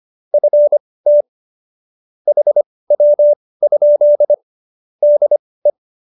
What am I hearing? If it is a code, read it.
Morse code, heard as a keyed tone, about 25 wpm, spaced as for 18 wpm, 590 Hz. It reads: FT HW? DE